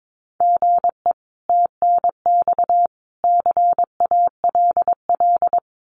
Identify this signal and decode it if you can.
Morse code, heard as a keyed tone, about 22 wpm, 700 Hz. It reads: GE TNX CALL
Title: Common QSO Phrases 22wpm